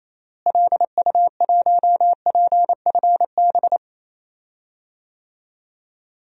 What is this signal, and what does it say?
Morse code, heard as a keyed tone, about 28 wpm, 700 Hz. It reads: LU1PFB
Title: DXCC-100 Call Signs - 6 Characters - Repeated 2x - 28wpm